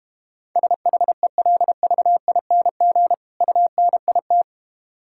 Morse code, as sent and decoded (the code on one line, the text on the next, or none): ... .... .-.. ...- .. -. --. / ..- -. .. -
SHELVING UNIT